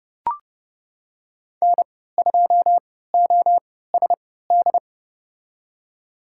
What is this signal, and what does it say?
Morse code, heard as a keyed tone, about 30 wpm, spaced as for 17 wpm, 700 Hz. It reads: N2OSD
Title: USA Call Signs - 5 Characters - 3x Character Spacing - 30wpm